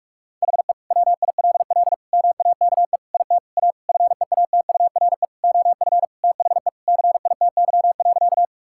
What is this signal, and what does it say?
Morse code, heard as a keyed tone, about 46 wpm, 700 Hz. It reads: HE WILL MAKE IT A FEATURE OF THE CITY.
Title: Sentences from Top 1000 Words - Review Part 5 - 45wpm